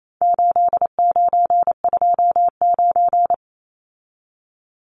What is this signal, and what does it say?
Morse code, heard as a keyed tone, about 28 wpm, 700 Hz. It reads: 8929
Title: Numbers - 4-Digit 28wpm